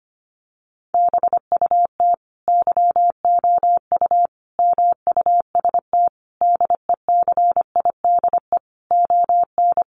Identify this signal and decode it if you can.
Morse code, heard as a keyed tone, about 25 wpm, 700 Hz. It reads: BUT YOU MUST DECIDE ON
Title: Sentences from Top 600 Words 25wpm